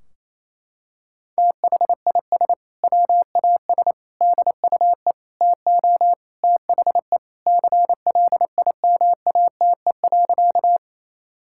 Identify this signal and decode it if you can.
Morse code, heard as a keyed tone, about 28 wpm, 700 Hz. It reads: THIS WAS DUE TO THE CLIMATE.
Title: Sentences from Top 1600 Words 28wpm